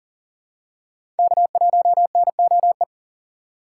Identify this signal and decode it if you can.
Morse code, heard as a keyed tone, about 40 wpm, 700 Hz. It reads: K1NOE
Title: USA Call Signs - 4x Character Spacing - 40wpm